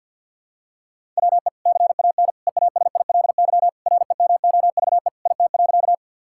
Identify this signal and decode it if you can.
Morse code, heard as a keyed tone, about 47 wpm, 700 Hz. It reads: WE CAN EASILY REMOVE IT.
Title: Sentences from Top 1000 Words - Review Part 5 - 50wpm